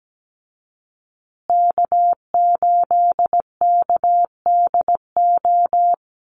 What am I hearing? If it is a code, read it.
Morse code, heard as a keyed tone, about 17 wpm, 700 Hz. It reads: K8KDO
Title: USA Call Signs Repeated 2x 17wpm